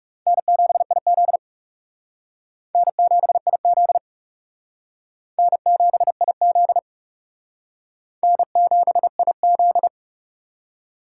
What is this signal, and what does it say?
Morse code, heard as a keyed tone, about 35 wpm, 700 Hz. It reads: N7IZ N7IZ N7IZ N7IZ
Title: Speed Racing - USA 1x2 Call Signs 30wpm